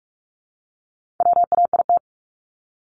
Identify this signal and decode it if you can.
Morse code, heard as a keyed tone, about 44 wpm, 700 Hz. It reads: WAIT
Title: ING Suffix - Top 100 Words with Base Words 45wpm